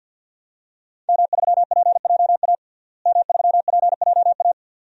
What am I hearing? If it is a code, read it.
Morse code, heard as a keyed tone, about 52 wpm, 700 Hz. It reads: M3PJA M3PJA
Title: DXCC-100 Call Signs - 5 Characters - Repeated 2x - 50wpm